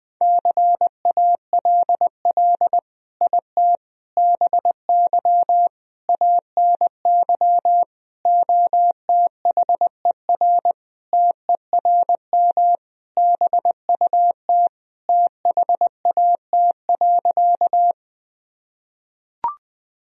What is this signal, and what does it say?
Morse code, heard as a keyed tone, about 19 wpm, 700 Hz. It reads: CALL IT BY ANY OTHER TERM BUT THAT.
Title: Sentences from Top 1000 Words - Review Part 4 - 20wpm